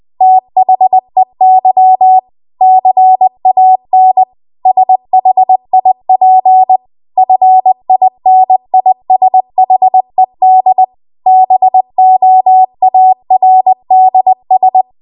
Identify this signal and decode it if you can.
Morse code, heard as a keyed tone, about 20 wpm, 750 Hz. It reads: THEY CAN SHIP FINISHED BOARDS
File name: Archive / 20 WPM / 20WPM